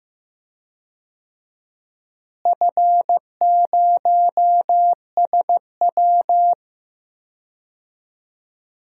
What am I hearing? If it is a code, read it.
Morse code, heard as a keyed tone, about 15 wpm, 700 Hz. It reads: F0SW
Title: DXCC-100 Call Signs - 4 Characters - Repeated 3x - 15wpm